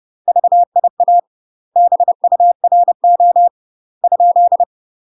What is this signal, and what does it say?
Morse code, heard as a keyed tone, about 30 wpm, 700 Hz. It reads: VIA BURO ?
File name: UK MASTER QSOs TRAINING / 30WPM / UK MASTER QSOs 0001